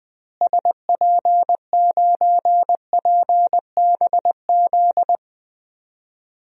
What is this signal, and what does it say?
Morse code, heard as a keyed tone, about 20 wpm, 700 Hz. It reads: SP9PBZ
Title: DXCC-100 Call Signs - 6 Characters - Repeated 2x - 20wpm